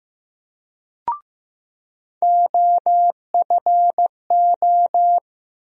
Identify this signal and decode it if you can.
Morse code, heard as a keyed tone, about 15 wpm, 700 Hz. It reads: OFO
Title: USA Call Sign Postfixes - 3 Letters 15wpm